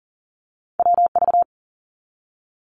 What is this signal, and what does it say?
Morse code, heard as a keyed tone, about 40 wpm, 700 Hz. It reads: WV
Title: US States Repeated 3x 40wpm